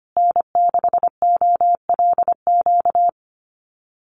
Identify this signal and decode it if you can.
Morse code, heard as a keyed tone, about 25 wpm, 700 Hz. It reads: N6OLQ
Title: USA Call Signs Repeated 3x 25wpm